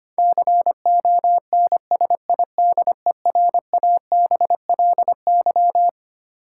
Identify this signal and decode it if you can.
Morse code, heard as a keyed tone, about 25 wpm, 700 Hz. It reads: CONSIDERABLY
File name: Twelve Letter Words - 25wpm